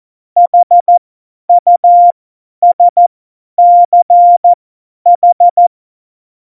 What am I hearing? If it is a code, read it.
Morse code, heard as a keyed tone, about 14 wpm, spaced as for 10 wpm, 700 Hz. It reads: HUSCH
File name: Morsekurs / Witze / Witz 19. Juni 2018, 50 F70